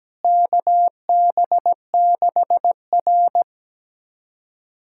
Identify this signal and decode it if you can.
Morse code, heard as a keyed tone, about 17 wpm, 700 Hz. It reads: KB6R
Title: USA Callsign DE Callsign - 4 Characters - 17wpm